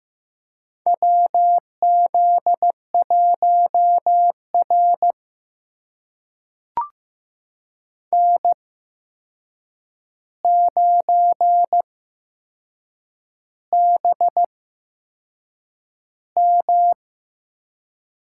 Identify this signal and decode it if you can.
Morse code, heard as a keyed tone, about 15 wpm, spaced as for 6 wpm, 700 Hz. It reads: WZ1R N9BM
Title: USA Call Signs - 4 Characters - 8x Character Spacing - 15wpm